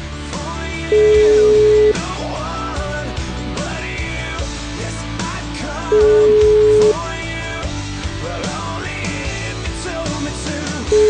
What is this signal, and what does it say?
Ringback tone (Europe, 425 Hz), with noise.